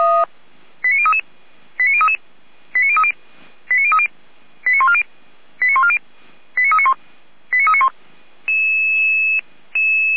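Five-tone selective call (ZVEI), heard as quick runs of five tones, tones of about 70 ms. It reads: 89033 89033 89030 89030 89140 89140 89391 89391
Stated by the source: ZVEI five-tone call 89042, received on SDR.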